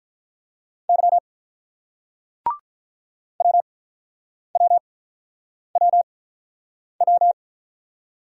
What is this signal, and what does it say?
Morse code, heard as a keyed tone, about 48 wpm, 700 Hz. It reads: Y W W W W